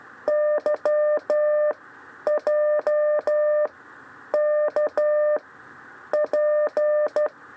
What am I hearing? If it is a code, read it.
Morse code, heard as a keyed tone, about 13 wpm, 600 Hz. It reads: Y J K P
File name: 6 lezione cw 2